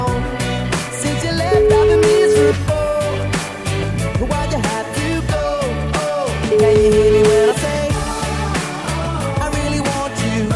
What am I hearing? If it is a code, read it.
Ringback tone (Europe, 425 Hz), with noise.